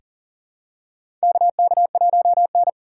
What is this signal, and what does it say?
Morse code, heard as a keyed tone, about 40 wpm, 700 Hz. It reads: KK1N
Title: USA Call Signs - 4 Characters - 3x Character Spacing - 40wpm